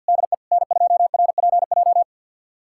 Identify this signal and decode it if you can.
Morse code, heard as a keyed tone, about 52 wpm, 700 Hz. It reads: DE N2RPJ
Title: USA Callsign DE Callsign - 5 Characters - 50wpm